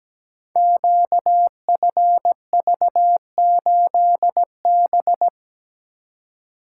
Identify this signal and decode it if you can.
Morse code, heard as a keyed tone, about 17 wpm, 700 Hz. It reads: QFV8B
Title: Code Groups - 5 Letter:Numbers - 17wpm